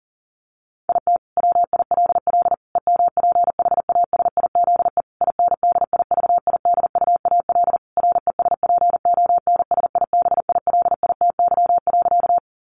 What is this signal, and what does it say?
Morse code, heard as a keyed tone, about 40 wpm, 700 Hz. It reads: IT WILL EMPHASIZE INDIVIDUAL RESPONSIBILITY.
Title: Sentences from Top 1500 Words 40wpm